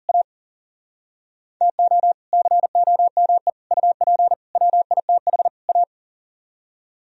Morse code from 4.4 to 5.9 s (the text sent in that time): WITH A